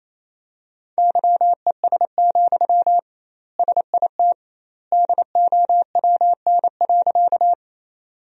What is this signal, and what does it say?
Morse code, heard as a keyed tone, about 28 wpm, 700 Hz. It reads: YES, SIT DOWN.
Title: 2x Word-Spacing - Sentences from Top 500 Words - 28wpm